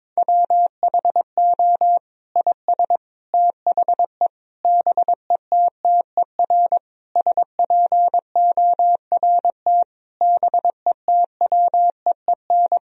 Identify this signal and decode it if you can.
Morse code, heard as a keyed tone, about 22 wpm, 700 Hz. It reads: WHO IS THE BETTER SPORT BETWEEN
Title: Sentences from Top 1000 Words - Review Part 4 - 22wpm